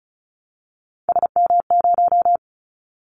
Morse code, heard as a keyed tone, about 36 wpm, 700 Hz. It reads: SM0
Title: Callsign Prefixes for Top 25 DXCC - 35wpm